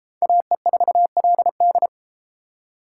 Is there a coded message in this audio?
Yes: Morse code from 0.2 to 1.9 s, heard as a keyed tone, about 31 wpm, 700 Hz. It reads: AE4LD